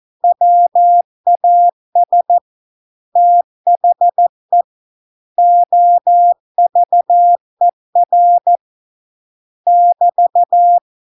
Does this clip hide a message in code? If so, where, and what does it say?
Yes: Morse code from 0.2 to 10.8 s, heard as a keyed tone, about 14 wpm, 700 Hz. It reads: WAS THE OVER <BT>